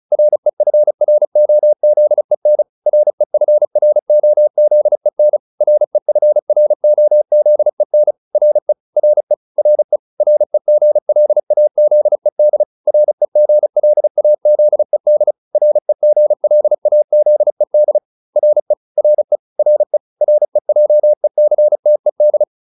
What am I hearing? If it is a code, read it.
Morse code, heard as a keyed tone, about 34 wpm, 600 Hz. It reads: REFROZEN REFROZEN REFROZEN RE RE RE REGLAZED REGLAZED REGLAZED RE RE RE REJECTED